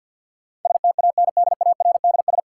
Morse code, heard as a keyed tone, about 53 wpm, 700 Hz. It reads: STANDARDS
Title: Sentences from Top 1000 Words - Review Part 6 - New Word Drill - 50wpm